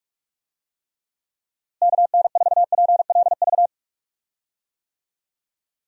Morse code, heard as a keyed tone, about 46 wpm, 700 Hz. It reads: KN4PLV
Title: USA Call Signs - 6 Characters - Repeated 3x - 45wpm